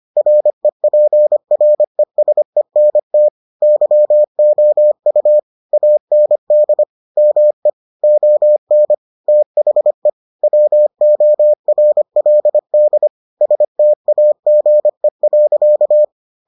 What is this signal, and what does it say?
Morse code, heard as a keyed tone, about 25 wpm, 600 Hz. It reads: REPRESENT YOU AND ME ON THE WORLD STAGE.